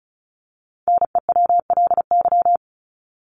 Morse code, heard as a keyed tone, about 34 wpm, 700 Hz. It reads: NEWLY
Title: LY Suffix - Top 100 Words with Base Words 35wpm